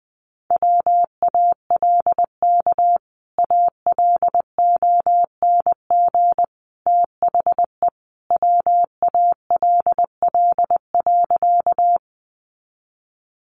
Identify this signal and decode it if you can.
Morse code, heard as a keyed tone, about 20 wpm, 700 Hz. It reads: WALK ALONG THE WALL.
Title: Sentences from Top 1100 Words 20wpm